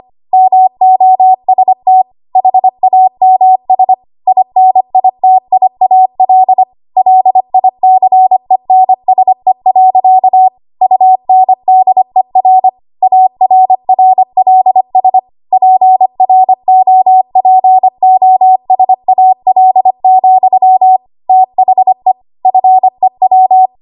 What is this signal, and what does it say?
Morse code, heard as a keyed tone, about 25 wpm, 750 Hz. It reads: MOST HAMS INITIAL LICENSE. UNDER ARRLS PROPOSAL, THE FEW
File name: Archive / 25 WPM / 25WPM